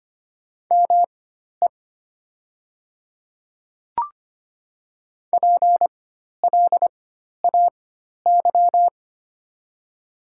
Morse code, heard as a keyed tone, about 25 wpm, spaced as for 12 wpm, 700 Hz. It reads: ME PLAY
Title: 4x Character Spacing - Top 200 Words - 25wpm